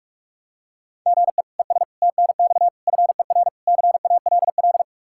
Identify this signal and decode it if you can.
Morse code, heard as a keyed tone, about 47 wpm, 700 Hz. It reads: GE ES TNX FER CALL
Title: Practice QSO - Exchange 1 - Beginning - 45wpm